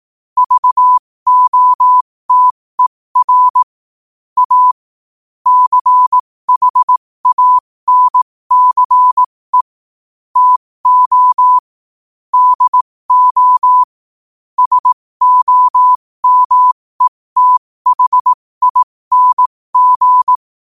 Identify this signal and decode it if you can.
Morse code, heard as a keyed tone, about 18 wpm, spaced as for 15 wpm, 1000 Hz. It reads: VOTER A CHANCE TO DO SOMETHING
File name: qotdfiles / 15wpm